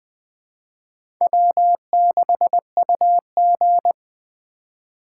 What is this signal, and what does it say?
Morse code, heard as a keyed tone, about 20 wpm, 700 Hz. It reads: W6UG
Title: USA Call Signs - 4 Characters - 3x Character Spacing - 20wpm